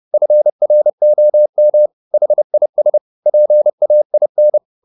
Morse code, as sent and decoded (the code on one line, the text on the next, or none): ..-. .-. --- -- / .... .. ... / .--. .- .. -.
FROM HIS PAIN